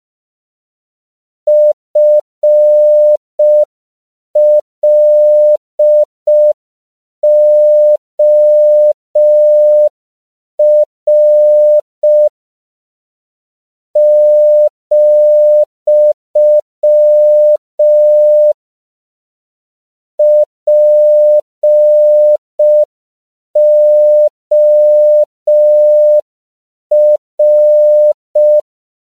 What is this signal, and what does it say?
Morse code, heard as a keyed tone, about 5 wpm, 600 Hz. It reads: FLOR , POR
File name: 5PPM